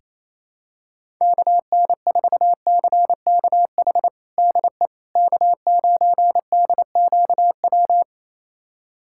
Morse code, heard as a keyed tone, about 28 wpm, 700 Hz. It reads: KN4CKH DE K9DQW
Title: USA Callsign DE Callsign 28wpm